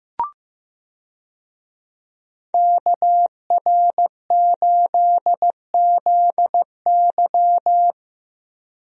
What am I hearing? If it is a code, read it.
Morse code, heard as a keyed tone, about 15 wpm, 700 Hz. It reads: KR8ZY
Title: USA Call Signs - 5 Characters - Repeated 3x - 15wpm